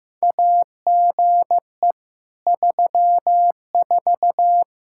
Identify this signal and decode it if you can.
Morse code, heard as a keyed tone, about 15 wpm, 700 Hz. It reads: AGE 34
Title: Practice QSO - Exchange 3 - More Info - 15wpm